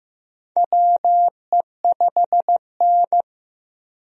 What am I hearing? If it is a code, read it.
Morse code, heard as a keyed tone, about 15 wpm, 700 Hz. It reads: WE5N
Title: USA Call Signs - 4 Characters - Repeated 2x - 15wpm